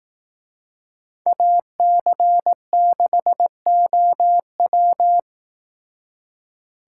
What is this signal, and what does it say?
Morse code, heard as a keyed tone, about 18 wpm, 700 Hz. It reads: AC6OW
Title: Speed Racing - USA Call Signs 15wpm